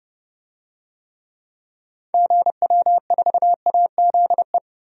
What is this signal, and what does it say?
Morse code, heard as a keyed tone, about 32 wpm, 700 Hz. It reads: GW4AZE
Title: DXCC-100 Call Signs - 6 Characters - 30wpm